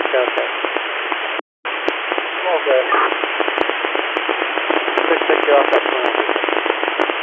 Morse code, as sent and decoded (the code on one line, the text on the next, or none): none